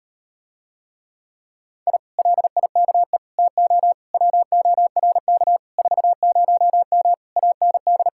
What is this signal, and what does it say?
Morse code, heard as a keyed tone, about 39 wpm, 700 Hz. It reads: I LIKE TO WORK 40M AND